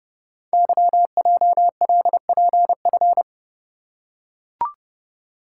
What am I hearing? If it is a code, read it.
Morse code, heard as a keyed tone, about 30 wpm, 700 Hz. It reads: YJLPF
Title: Code Groups - 5 Letters - 30wpm